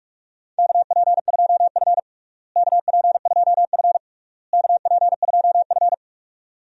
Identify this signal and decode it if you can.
Morse code, heard as a keyed tone, about 46 wpm, 700 Hz. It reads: KP2F KP2F KP2F
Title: DXCC-100 Call Signs - 4 Characters - Repeated 3x - 45wpm